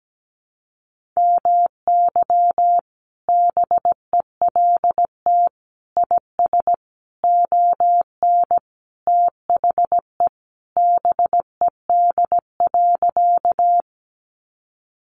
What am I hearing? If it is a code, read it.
Morse code, heard as a keyed tone, about 17 wpm, 700 Hz. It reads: MY BELT IS ON THE BED.
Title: Sentences from Top 1400 Words 17wpm